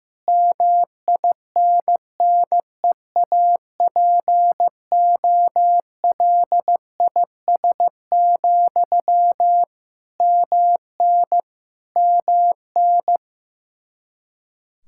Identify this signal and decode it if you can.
Morse code, heard as a keyed tone, about 15 wpm, 700 Hz. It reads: MINNEAPOLIS, MN MN
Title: Practice QSO - Exchange 1 - Beginning - 15wpm